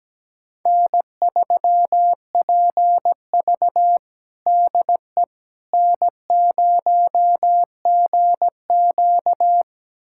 Morse code, heard as a keyed tone, about 17 wpm, 700 Hz. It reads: N3PV DE N0GQ
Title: USA Callsign DE Callsign - 4 Characters - 17wpm